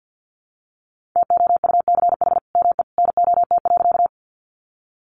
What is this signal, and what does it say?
Morse code, heard as a keyed tone, about 50 wpm, 700 Hz. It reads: TOUCH ME NOT.
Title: Sentences from Top 1200 Words 50wpm